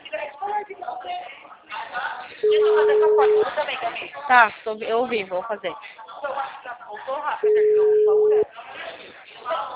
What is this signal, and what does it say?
Ringback tone (Europe, 425 Hz), with speech.
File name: out- -2016- - - .93229